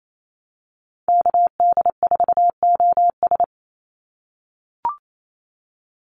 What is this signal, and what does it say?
Morse code, heard as a keyed tone, about 28 wpm, 700 Hz. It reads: KD4OS
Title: Call Signs - 28wpm